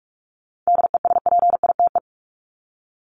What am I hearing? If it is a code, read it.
Morse code, heard as a keyed tone, about 44 wpm, 700 Hz. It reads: DESPITE
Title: Top 751-1000 Words 45wpm